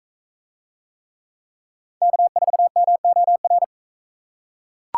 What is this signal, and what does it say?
Morse code, heard as a keyed tone, about 40 wpm, 700 Hz. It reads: K4MOR